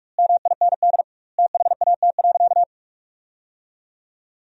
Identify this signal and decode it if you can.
Morse code, heard as a keyed tone, about 41 wpm, 700 Hz. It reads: MIND THAT.
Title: 2x Word-Spacing - Sentences from Top 500 Words - 45wpm